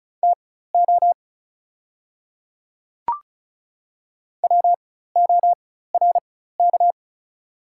Morse code, heard as a keyed tone, about 36 wpm, spaced as for 16 wpm, 700 Hz. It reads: TO WORK
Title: 4x Character Spacing - Top 500 Words - 35wpm